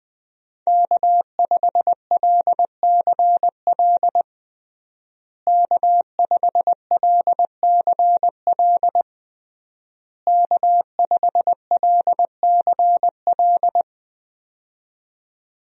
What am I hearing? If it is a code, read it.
Morse code, heard as a keyed tone, about 21 wpm, 700 Hz. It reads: K5LCL K5LCL K5LCL
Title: USA Call Signs - 5 Characters - Repeated 3x - 20wpm